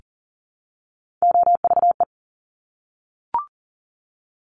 Morse code, heard as a keyed tone, about 39 wpm, 700 Hz. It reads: OVE
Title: Top Three Letter Combinations in Words 40wpm